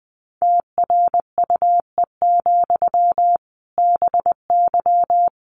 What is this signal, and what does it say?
Morse code, heard as a keyed tone, about 20 wpm, 700 Hz. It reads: TRUE, BY
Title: Sentences from Top 1500 Words 20wpm